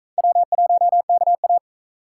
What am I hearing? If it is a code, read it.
Morse code, heard as a keyed tone, about 42 wpm, 700 Hz. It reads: W1KA